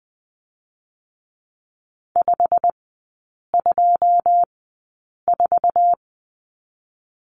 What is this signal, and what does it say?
Morse code, heard as a keyed tone, about 20 wpm, spaced as for 8 wpm, 690 Hz. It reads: 524
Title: 087 Lesson 15e - Number 5 - Three Characters 20wpm